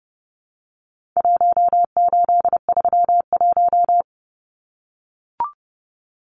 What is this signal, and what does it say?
Morse code, heard as a keyed tone, about 30 wpm, 700 Hz. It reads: 1831
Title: Numbers - Contest Style - 4-Digit 30wpm